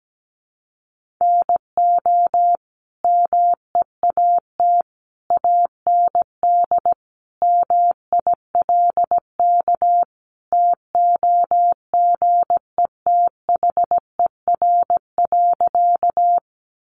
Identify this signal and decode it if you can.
Morse code, heard as a keyed tone, about 17 wpm, 700 Hz. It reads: NO MEAT AND MILK TOGETHER.
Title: Sentences from Top 1500 Words 17wpm